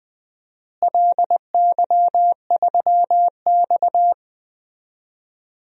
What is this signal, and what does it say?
Morse code, heard as a keyed tone, about 21 wpm, 700 Hz. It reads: LY3X
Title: DXCC-100 Call Signs - 4 Characters - Repeated 3x - 20wpm